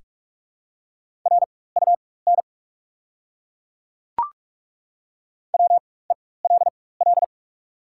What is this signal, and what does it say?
Morse code, heard as a keyed tone, about 42 wpm, spaced as for 21 wpm, 700 Hz. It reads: RUN WELL
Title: 4x Character Spacing - Top 500 Words - 45wpm